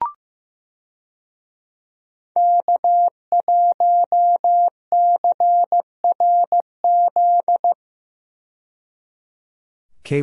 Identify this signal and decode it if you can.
Morse code, heard as a keyed tone, about 15 wpm, 700 Hz. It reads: K1CRZ
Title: USA Call Signs - 5 Characters - 15wpm